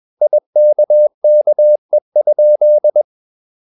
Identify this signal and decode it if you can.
Morse code, heard as a keyed tone, about 21 wpm, 600 Hz. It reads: IKKE?
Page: Ferie 20 wpm